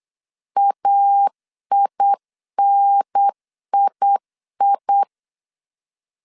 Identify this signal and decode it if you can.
Morse code, heard as a keyed tone, about 8 wpm, 800 Hz. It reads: AINII